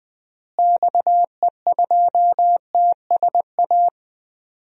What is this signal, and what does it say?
Morse code, heard as a keyed tone, about 20 wpm, 700 Hz. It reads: XE2TSA
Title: DXCC-100 Call Signs - 6 Characters - Repeated 2x - 20wpm